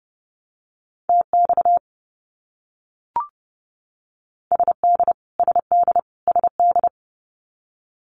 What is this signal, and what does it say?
Morse code, heard as a keyed tone, about 30 wpm, 700 Hz. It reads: TX SD SD SD
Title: US States Repeated 3x 30wpm